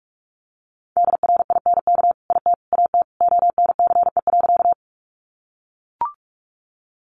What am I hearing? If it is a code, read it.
Morse code, heard as a keyed tone, about 44 wpm, 700 Hz. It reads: DRINK IT AT ONCE.
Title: Sentences from Top 1500 Words 45wpm